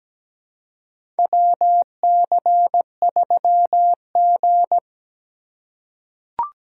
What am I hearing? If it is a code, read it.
Morse code, heard as a keyed tone, about 17 wpm, 700 Hz. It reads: WC3G
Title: USA Call Signs - 4 Characters - 4x Character Spacing - 17wpm